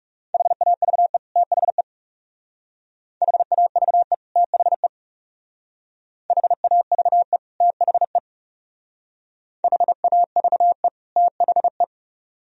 Morse code, heard as a keyed tone, about 39 wpm, 700 Hz. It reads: HAVE THE HAVE THE HAVE THE HAVE THE